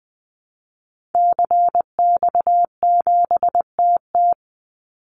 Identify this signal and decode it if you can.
Morse code, heard as a keyed tone, about 20 wpm, 700 Hz. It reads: CX7TT
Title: Call Signs Repeated 3x - 20wpm